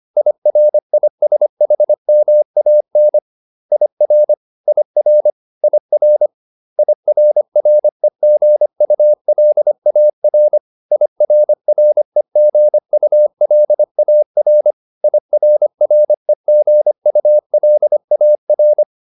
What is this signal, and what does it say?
Morse code, heard as a keyed tone, about 25 wpm, 600 Hz. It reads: IRISHMAN IR IR IR IRREGULAR IRREGULAR IRREGULAR
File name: IR8 25